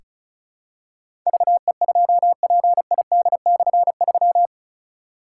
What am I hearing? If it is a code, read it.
Morse code, heard as a keyed tone, about 36 wpm, 700 Hz. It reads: VE2PID/3
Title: K1USN SST QSO Practice - 35wpm